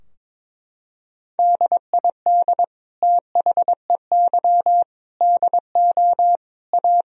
Morse code, heard as a keyed tone, about 22 wpm, 700 Hz. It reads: DID THEY DO A
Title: Sentences from Top 1000 Words - Review Part 3 - 22wpm